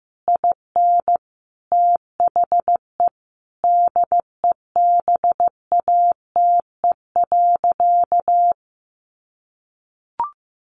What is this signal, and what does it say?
Morse code, heard as a keyed tone, about 14 wpm, 700 Hz. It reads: IN THE DEBATE.
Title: Sentences from Top 1500 Words 15wpm